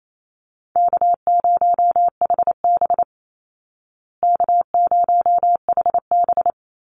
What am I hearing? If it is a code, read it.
Morse code, heard as a keyed tone, about 28 wpm, 700 Hz. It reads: K0HB K0HB
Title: Call Signs Repeated 2x - 28wpm